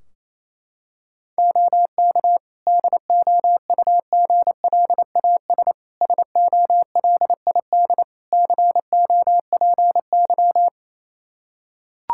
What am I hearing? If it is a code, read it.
Morse code, heard as a keyed tone, about 28 wpm, 700 Hz. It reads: OK DOUGLAS SOLID COPY
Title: Practice QSO - Exchange 3 - More Info - 28wpm